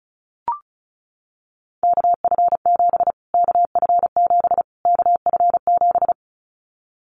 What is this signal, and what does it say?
Morse code, heard as a keyed tone, about 34 wpm, 700 Hz. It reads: KF7 KF7 KF7
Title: Callsign Prefixes for Top 25 DXCC - Repeated 3x - 35wpm